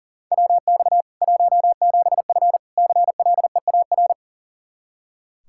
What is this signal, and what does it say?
Morse code, heard as a keyed tone, about 40 wpm, 700 Hz. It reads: WX 17F CLEAR
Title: Practice QSO - Exchange 2 - Info - 40wpm